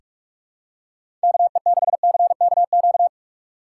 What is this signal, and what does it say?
Morse code, heard as a keyed tone, about 43 wpm, 700 Hz. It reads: KE6CKQ
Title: USA Call Signs - 6 Characters - 2x Character Spacing - 45wpm